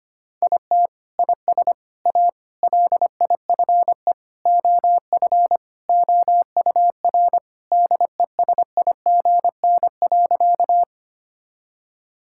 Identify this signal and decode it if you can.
Morse code, heard as a keyed tone, about 25 wpm, 700 Hz. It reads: IT IS A LIFE OF OUR DESIGN.
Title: Sentences from Top 1000 Words - Review Part 5 - 25wpm